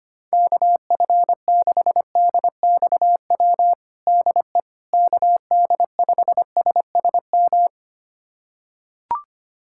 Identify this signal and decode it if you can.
Morse code, heard as a keyed tone, about 25 wpm, 700 Hz. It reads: KF6DXW DE KD5SSM
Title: USA Callsign DE Callsign - 6 Characters - 25wpm